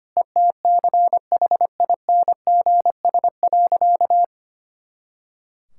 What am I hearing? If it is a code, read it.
Morse code, heard as a keyed tone, about 25 wpm, 700 Hz. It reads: ETCHINGS.
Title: Pangram Challenge v2 - 25wpm